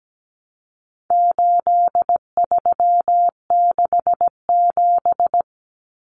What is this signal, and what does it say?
Morse code, heard as a keyed tone, about 17 wpm, 700 Hz. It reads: 8367
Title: Common Cut Numbers - 4-Digit 17wpm